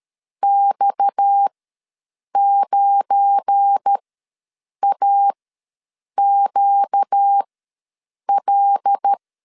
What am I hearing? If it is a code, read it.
Morse code, heard as a keyed tone, about 13 wpm, spaced as for 7 wpm, 800 Hz. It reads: X9AQL
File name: LEZIONI / 7 GF9Z / 7 GF9 ws 12wpm 3344